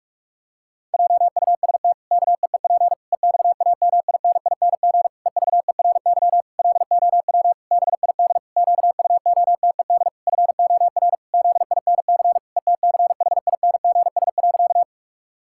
Morse code, heard as a keyed tone, about 46 wpm, 700 Hz. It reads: JUST KEEP EXAMINING EVERY LOW BID QUOTED FOR ZINC ETCHINGS.
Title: Pangram Challenge v2 - 45wpm